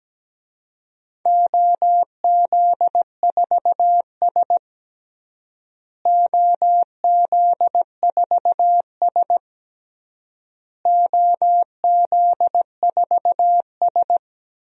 Morse code, heard as a keyed tone, about 17 wpm, 700 Hz. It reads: OZ4S OZ4S OZ4S
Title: DXCC-100 Call Signs - 4 Characters - Repeated 3x - 17wpm